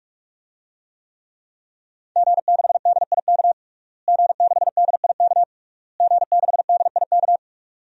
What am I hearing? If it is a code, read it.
Morse code, heard as a keyed tone, about 41 wpm, 700 Hz. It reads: G6DIK G6DIK G6DIK
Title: DXCC-100 Call Signs - 5 Characters - Repeated 3x - 45wpm